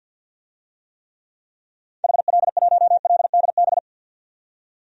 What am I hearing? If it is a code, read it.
Morse code, heard as a keyed tone, about 47 wpm, 700 Hz. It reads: HL1LDB
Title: DXCC-100 Call Signs - 6 Characters - 50wpm